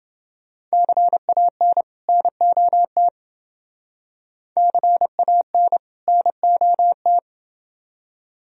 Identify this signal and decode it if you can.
Morse code, heard as a keyed tone, about 28 wpm, 700 Hz. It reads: CAN NOT CAN NOT